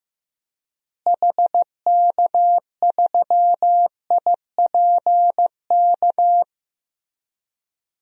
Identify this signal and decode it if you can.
Morse code, heard as a keyed tone, about 15 wpm, 700 Hz. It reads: HK3IPK
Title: DXCC-100 Call Signs - 6 Characters - Repeated 3x - 15wpm